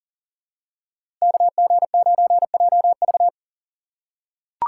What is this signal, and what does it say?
Morse code, heard as a keyed tone, about 40 wpm, 700 Hz. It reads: KG9JV
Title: USA Call Signs - 5 Characters - 6x Character Spacing - 40wpm